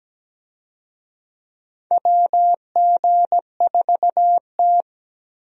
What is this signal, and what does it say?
Morse code, heard as a keyed tone, about 17 wpm, 700 Hz. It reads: WG4T
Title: USA Call Signs - 4 Characters - Repeated 2x - 17wpm